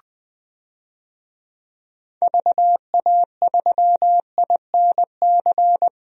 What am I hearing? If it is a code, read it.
Morse code, heard as a keyed tone, about 20 wpm, 700 Hz. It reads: VA3INC
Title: DXCC-100 Call Signs - 6 Characters - 20wpm